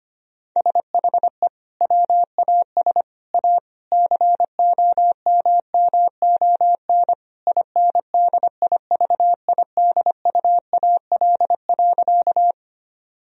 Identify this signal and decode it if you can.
Morse code, heard as a keyed tone, about 25 wpm, 700 Hz. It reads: SHE WAS A COMMON INDIVIDUAL.
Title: Sentences from Top 1000 Words - Review Part 4 - 25wpm